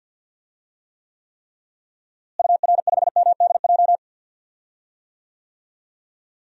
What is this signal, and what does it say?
Morse code, heard as a keyed tone, about 48 wpm, 700 Hz. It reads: UR5MDJ
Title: DXCC-100 Call Signs - 6 Characters - Repeated 3x - 50wpm